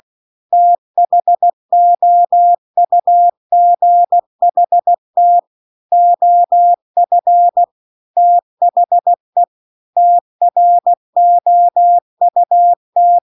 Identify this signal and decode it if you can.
Morse code, heard as a keyed tone, about 16 wpm, 700 Hz. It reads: THOUGHT OF THE TROUT